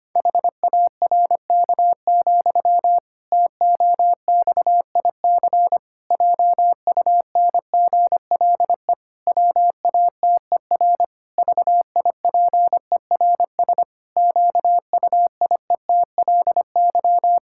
Morse code, heard as a keyed tone, about 25 wpm, 700 Hz. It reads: HARK, TOXIC JUNGLE WATER VIPERS QUIETLY
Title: Pangram Challenge v2 - 25wpm